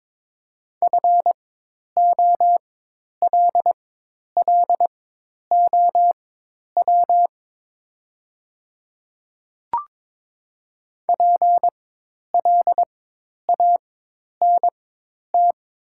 Morse code, heard as a keyed tone, about 22 wpm, spaced as for 10 wpm, 700 Hz. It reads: FOLLOW PLANT
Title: 4x Character Spacing - Top 300 Words - 22wpm